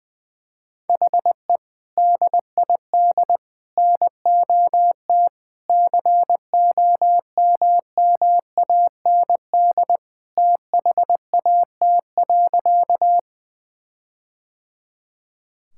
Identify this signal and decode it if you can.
Morse code, heard as a keyed tone, about 20 wpm, 700 Hz. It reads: HE DID NOT COMMAND THAT.
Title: Sentences from Top 1600 Words 20wpm